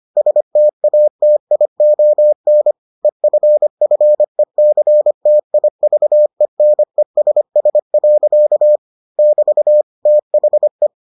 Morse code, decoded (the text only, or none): STATION EFFECTIVENESS. <BT> THE